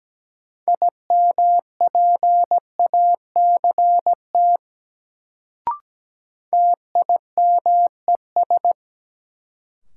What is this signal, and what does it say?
Morse code, heard as a keyed tone, about 17 wpm, 700 Hz. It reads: IMPACT TIMES
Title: Sentences from Top 1000 Words - Review Part 6 - New Word Drill - 17wpm